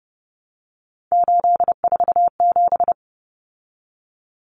Morse code, heard as a keyed tone, about 30 wpm, 700 Hz. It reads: 847
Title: Numbers - 3-Digit 30wpm